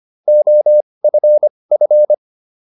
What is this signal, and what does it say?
Morse code, heard as a keyed tone, about 25 wpm, spaced as for 20 wpm, 600 Hz. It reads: OFF